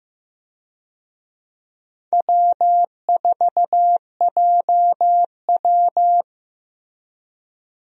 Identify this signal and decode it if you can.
Morse code, heard as a keyed tone, about 15 wpm, 700 Hz. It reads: W4JW
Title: USA Call Signs - 4 Characters - Repeated 3x - 15wpm